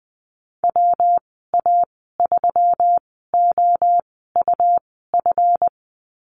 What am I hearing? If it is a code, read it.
Morse code, heard as a keyed tone, about 20 wpm, spaced as for 14 wpm, 700 Hz. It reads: WA3OUF
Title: 193 Lesson 31g - Letter v - Call Signs 20wpm